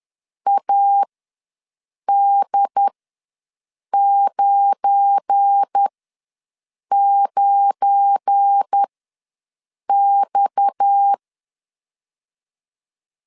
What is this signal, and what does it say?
Morse code, heard as a keyed tone, about 10 wpm, spaced as for 6 wpm, 800 Hz. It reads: AD99X
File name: LEZIONI / 7 GF9Z / 7 GF9 ws 10wpm 4034